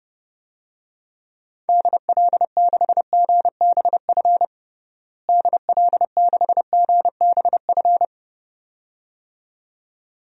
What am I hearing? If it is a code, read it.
Morse code, heard as a keyed tone, about 31 wpm, 700 Hz. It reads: DL6GBF DL6GBF
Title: DXCC-100 Call Signs - 6 Characters - Repeated 2x - 30wpm